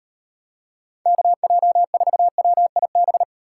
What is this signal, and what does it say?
Morse code, heard as a keyed tone, about 31 wpm, 700 Hz. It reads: KJ4WIB
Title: Speed Racing - USA 2x3 Call Signs 25wpm